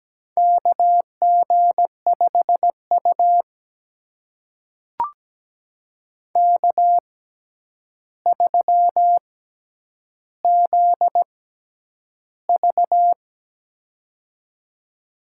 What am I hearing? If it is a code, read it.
Morse code, heard as a keyed tone, about 17 wpm, spaced as for 9 wpm, 700 Hz. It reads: KG5U K3ZV
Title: USA Call Signs - 6x Character Spacing - 17wpm